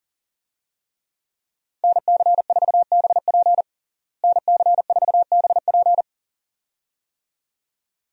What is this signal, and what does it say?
Morse code, heard as a keyed tone, about 37 wpm, 700 Hz. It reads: NC4BP NC4BP
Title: USA Call Signs - 5 Characters - Repeated 2x - 40wpm